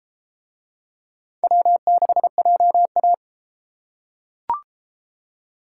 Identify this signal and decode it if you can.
Morse code, heard as a keyed tone, about 33 wpm, 700 Hz. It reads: W6JA